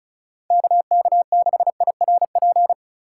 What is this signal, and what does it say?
Morse code, heard as a keyed tone, about 34 wpm, 700 Hz. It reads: KK6IRP
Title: USA Call Signs - 3x Character Spacing - 35wpm